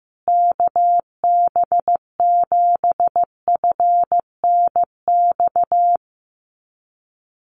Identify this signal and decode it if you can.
Morse code, heard as a keyed tone, about 15 wpm, 700 Hz. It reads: KB7FNX
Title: USA Call Signs Repeated 2x 15wpm